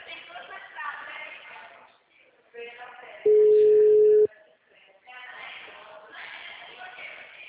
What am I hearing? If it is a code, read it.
Ringback tone (Europe, 425 Hz), with speech.